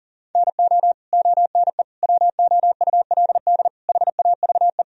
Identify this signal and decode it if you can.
Morse code, heard as a keyed tone, about 37 wpm, 700 Hz. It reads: NO ONE WOULD HAVE
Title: Sentences from Top 1000 Words - Review Part 1 - 40wpm